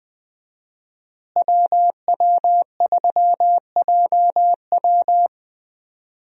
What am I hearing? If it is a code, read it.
Morse code, heard as a keyed tone, about 20 wpm, 700 Hz. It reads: WW3JW
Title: USA Call Signs - 5 Characters - Repeated 3x - 20wpm